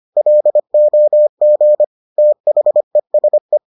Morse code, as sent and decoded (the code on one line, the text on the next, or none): .-.. --- --. / - .... ... .
LOG THESE